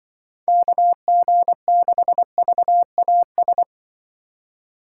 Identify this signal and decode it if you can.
Morse code, heard as a keyed tone, about 24 wpm, 700 Hz. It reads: KG6VAS
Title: Speed Racing - USA Call Signs 20wpm